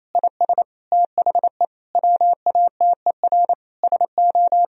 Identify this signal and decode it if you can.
Morse code, heard as a keyed tone, about 27 wpm, 700 Hz. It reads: IS THE WATER SO
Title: Sentences from Top 1000 Words - Review Part 2 - 28wpm